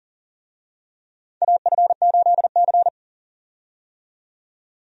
Morse code, heard as a keyed tone, about 40 wpm, 700 Hz. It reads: AF8C
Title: USA Call Signs - 4 Characters - 40wpm